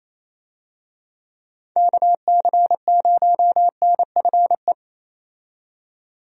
Morse code, heard as a keyed tone, about 28 wpm, 700 Hz. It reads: KC0NFE
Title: USA Call Signs - 6 Characters - 28wpm